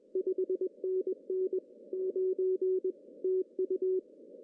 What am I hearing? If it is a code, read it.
Morse code, heard as a keyed tone, about 21 wpm, 1900 Hz. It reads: 5NN 9 TU
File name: CW / cqww-cw-k5tr